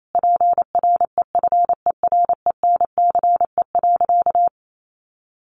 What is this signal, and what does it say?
Morse code, heard as a keyed tone, about 28 wpm, 700 Hz. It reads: PREFERENCE.
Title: Sentences from Top 1500 Words 28wpm